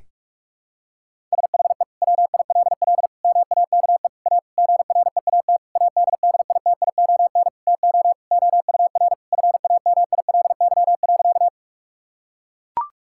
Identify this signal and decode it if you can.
Morse code, heard as a keyed tone, about 41 wpm, 700 Hz. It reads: SHE WILL MAKE A GREAT ADDITION TO OUR FAMILY.
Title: Sentences from Top 1000 Words - Review Part 6 - 45wpm